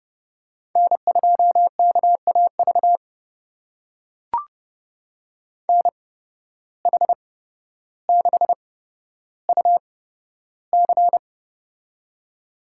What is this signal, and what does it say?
Morse code, heard as a keyed tone, about 32 wpm, spaced as for 13 wpm, 700 Hz. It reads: N2KAV NH6UC
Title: USA Call Signs - 5 Characters - 8x Character Spacing - 30wpm